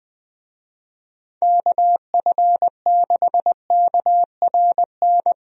Morse code, heard as a keyed tone, about 20 wpm, 700 Hz. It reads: KF6KRN
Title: USA Call Signs - 6 Characters - Repeated 2x - 20wpm